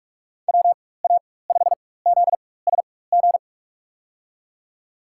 Morse code, heard as a keyed tone, about 46 wpm, spaced as for 21 wpm, 700 Hz. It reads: WA5ZSG
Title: USA Call Signs - 6 Characters - 4x Character Spacing - 45wpm